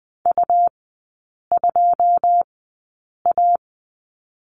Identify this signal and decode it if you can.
Morse code, heard as a keyed tone, about 20 wpm, spaced as for 8 wpm, 700 Hz. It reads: U2A
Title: 166 Lesson 27e - Number 7 - Three Characters 20wpm